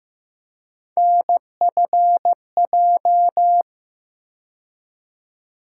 Morse code, heard as a keyed tone, about 15 wpm, 700 Hz. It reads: NFJ
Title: USA Call Sign Postfixes - 3 Letters 15wpm